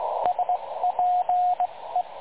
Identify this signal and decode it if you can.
Morse code, heard as a keyed tone, about 19 wpm, spaced as for 16 wpm, 720 Hz, with radio noise. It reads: SPE
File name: dlazaawansowanych / nagraniazpasma / websdr recording 3565.0kHz